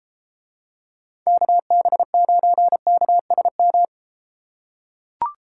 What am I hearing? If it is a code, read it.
Morse code, heard as a keyed tone, about 33 wpm, 700 Hz. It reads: KB9KSM